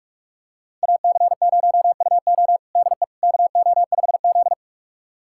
Morse code, heard as a keyed tone, about 46 wpm, 700 Hz. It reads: AC0UO DE KO5Z
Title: USA Callsign DE Callsign 45wpm